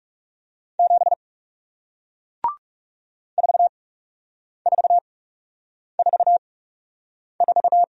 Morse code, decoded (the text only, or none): 7 4444